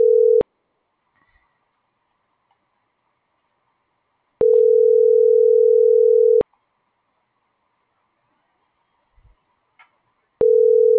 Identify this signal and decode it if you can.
Ringback tone (North America), 440 + 480 Hz.